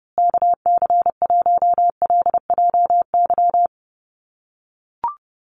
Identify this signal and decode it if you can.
Morse code, heard as a keyed tone, about 30 wpm, 700 Hz. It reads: KC1LJY
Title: USA Call Signs Repeated 3x 30wpm